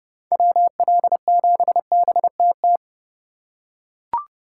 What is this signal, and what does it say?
Morse code, heard as a keyed tone, about 30 wpm, 700 Hz. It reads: WL7BTT
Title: USA Call Signs - 6 Characters - 4x Character Spacing - 30wpm